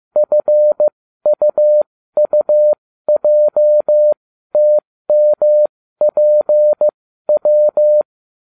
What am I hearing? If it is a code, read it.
Morse code, heard as a keyed tone, about 15 wpm, spaced as for 12 wpm, 600 Hz. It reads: FUUJTMPW